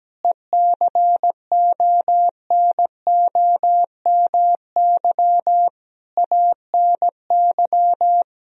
Morse code, heard as a keyed tone, about 17 wpm, 700 Hz. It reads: ECONOMY ANY
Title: Sentences from Top 1000 Words - Review Part 4 - 17wpm